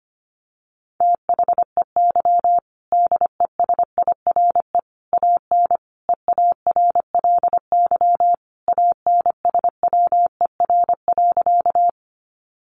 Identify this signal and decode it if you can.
Morse code, heard as a keyed tone, about 25 wpm, 700 Hz. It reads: THEY DESIRE AN EARLY ANSWER.
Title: Sentences from Top 1100 Words 25wpm